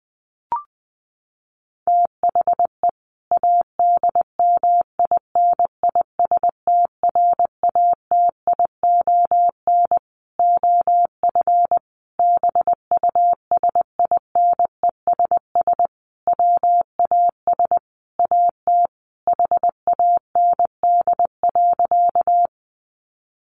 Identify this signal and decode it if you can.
Morse code, heard as a keyed tone, about 20 wpm, 700 Hz. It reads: THE ADMINISTRATION OF BUSINESS WAS AT HAND.
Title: Sentences from Top 1300 Words 20wpm